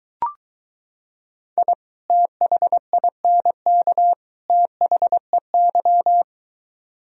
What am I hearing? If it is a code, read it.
Morse code, heard as a keyed tone, about 23 wpm, 700 Hz. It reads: I THINK THEY